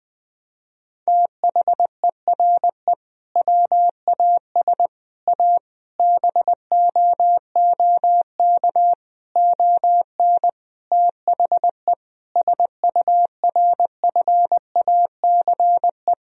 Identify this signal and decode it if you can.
Morse code, heard as a keyed tone, about 20 wpm, 700 Hz. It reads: THERE WAS A BOOK ON THE SURFACE
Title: Sentences from Top 1000 Words - Review Part 3 - 20wpm